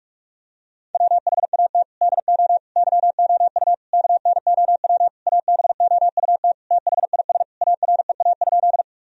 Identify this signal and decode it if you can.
Morse code, heard as a keyed tone, about 46 wpm, 700 Hz. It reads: WHAT DO YOU KNOW ABOUT THIS AREA?
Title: Sentences from Top 1000 Words - Review Part 2 - 45wpm